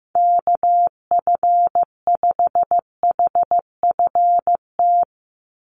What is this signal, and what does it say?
Morse code, heard as a keyed tone, about 15 wpm, 700 Hz. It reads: KF5HFT
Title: USA Call Signs 15wpm